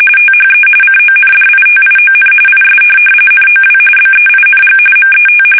RTTY, about 45.45 baud. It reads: THE CONDITION IS NOW BEING MET
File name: fsk850